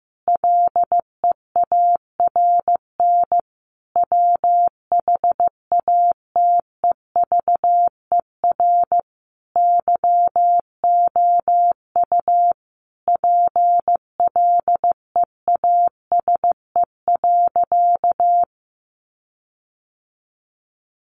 Morse code, heard as a keyed tone, about 15 wpm, 700 Hz. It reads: LEARN WHATEVER YOU PLEASE.
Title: Sentences from Top 1500 Words 15wpm